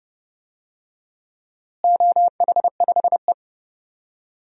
Morse code, heard as a keyed tone, about 32 wpm, 700 Hz. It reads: OH5E
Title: DXCC-100 Call Signs - 4 Characters - Repeated 2x - 30wpm